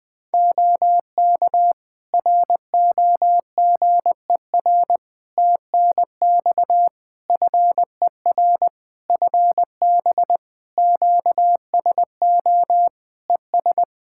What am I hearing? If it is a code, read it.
Morse code, heard as a keyed tone, about 21 wpm, 700 Hz. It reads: OK ROGER TNX FER FB QSO ES
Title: Practice QSO - Exchange 4 - Ending - 20wpm